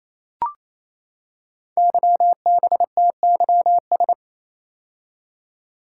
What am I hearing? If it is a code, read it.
Morse code, heard as a keyed tone, about 27 wpm, 700 Hz. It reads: YBTYS
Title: Code Groups - 5 Letters - 28wpm